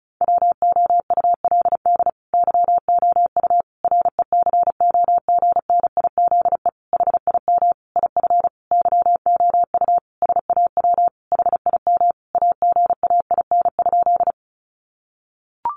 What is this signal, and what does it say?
Morse code, heard as a keyed tone, about 34 wpm, 700 Hz. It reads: WOULD YOU RECOGNIZE HIM IF YOU SAW HIM AGAIN?
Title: Sentences from Top 1400 Words 35wpm